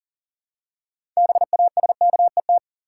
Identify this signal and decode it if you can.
Morse code, heard as a keyed tone, about 40 wpm, 700 Hz. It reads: BASKET
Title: Six Letter Words - 40wpm